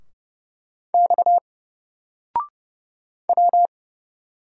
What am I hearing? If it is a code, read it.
Morse code, heard as a keyed tone, about 30 wpm, 700 Hz. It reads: X W
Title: Single Character (Letter:Number:Punctuation:Prosign) Alt 30wpm